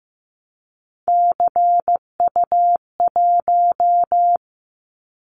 Morse code, heard as a keyed tone, about 15 wpm, 700 Hz. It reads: CU1
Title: Callsign Prefixes for Top 100 DXCC - 15wpm